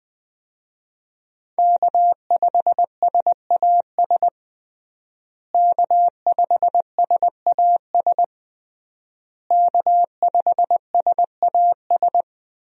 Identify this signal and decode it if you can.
Morse code, heard as a keyed tone, about 20 wpm, 700 Hz. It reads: K5SAS K5SAS K5SAS
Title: USA Call Signs - 5 Characters - Repeated 3x - 20wpm